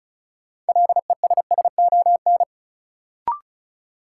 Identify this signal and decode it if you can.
Morse code, heard as a keyed tone, about 36 wpm, 700 Hz. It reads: LESSON
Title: Sentences from Top 1200 Words - New Word Drill - 35wpm